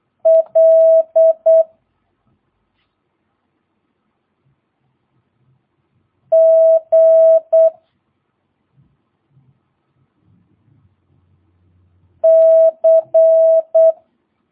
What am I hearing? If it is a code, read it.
Morse code, heard as a keyed tone, about 8 wpm, 650 Hz. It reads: L G C
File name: jh8cbh / 2022cw